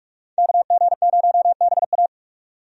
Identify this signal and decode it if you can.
Morse code, heard as a keyed tone, about 41 wpm, 700 Hz. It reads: KG0BA
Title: USA Call Signs - 5 Characters - 45wpm